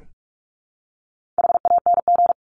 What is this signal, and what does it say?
Morse code, heard as a keyed tone, about 44 wpm, 700 Hz. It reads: HANG
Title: Top 1501-1750 Words 45wpm